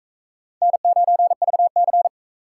Morse code, heard as a keyed tone, about 39 wpm, 700 Hz. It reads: N9VC